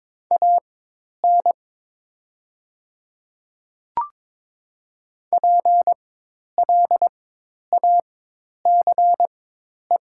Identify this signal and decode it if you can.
Morse code, heard as a keyed tone, about 22 wpm, spaced as for 10 wpm, 700 Hz. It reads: AN PLACE
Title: 4x Character Spacing - Top 300 Words - 22wpm